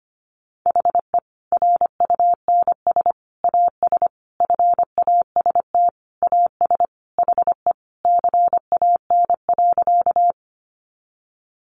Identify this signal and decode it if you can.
Morse code, heard as a keyed tone, about 25 wpm, 700 Hz. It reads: HE RUNS AS FAST AS HE CAN.
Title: Additional Easy Phrases 25wpm